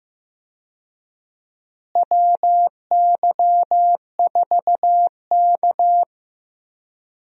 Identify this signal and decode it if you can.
Morse code, heard as a keyed tone, about 15 wpm, 700 Hz. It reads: WY4K
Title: USA Call Signs - 4 Characters - Repeated 2x - 15wpm